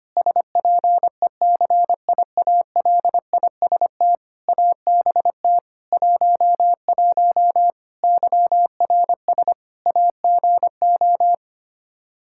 Morse code, heard as a keyed tone, about 25 wpm, 700 Hz. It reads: SPECIALIST ABT 11 YRS AGO
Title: Practice QSO - Exchange 3 - More Info - 25wpm